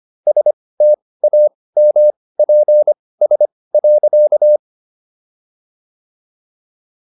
Morse code, heard as a keyed tone, about 25 wpm, spaced as for 18 wpm, 600 Hz. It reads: STAMPS.